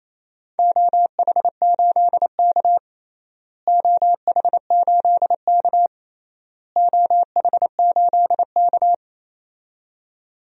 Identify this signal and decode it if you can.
Morse code, heard as a keyed tone, about 28 wpm, 700 Hz. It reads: OH8K OH8K OH8K
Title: DXCC-100 Call Signs - 4 Characters - Repeated 3x - 28wpm